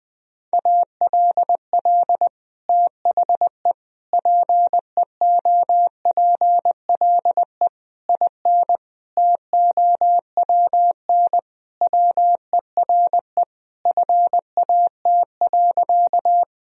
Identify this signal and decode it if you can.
Morse code, heard as a keyed tone, about 20 wpm, 700 Hz. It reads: ALL THE PEOPLE IN TOWN WERE FAT.
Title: Sentences from Top 1000 Words - Review Part 4 - 20wpm